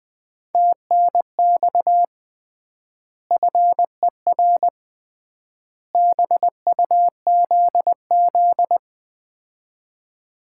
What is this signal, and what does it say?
Morse code, heard as a keyed tone, about 21 wpm, 700 Hz. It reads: TNX FER BUZZ
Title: Common QSO Phrases - 3x Word-Spacing - 20wpm